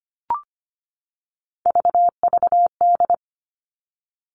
Morse code, heard as a keyed tone, about 25 wpm, 700 Hz. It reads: VVD
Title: Three Challenging Letter-Number Combinations 25wpm